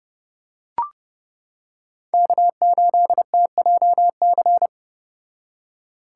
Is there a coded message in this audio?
Yes: Morse code from 2.1 to 4.7 s, heard as a keyed tone, about 30 wpm, 700 Hz. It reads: K8TJC